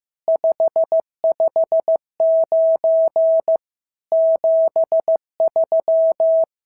Morse code, heard as a keyed tone, about 14 wpm, 650 Hz. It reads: 559 73
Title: SOTA QSO Practice 15wpm